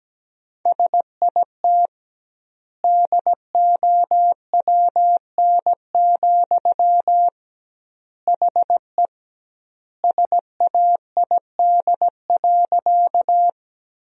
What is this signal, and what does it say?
Morse code, heard as a keyed tone, about 17 wpm, 700 Hz. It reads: SIT DOWN, HE SAID.
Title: 2x Word-Spacing - Sentences from Top 500 Words - 17wpm